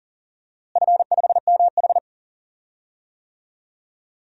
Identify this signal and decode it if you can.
Morse code, heard as a keyed tone, about 40 wpm, 700 Hz. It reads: F5MH
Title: DXCC-100 Call Signs - 4 Characters - Repeated 3x - 40wpm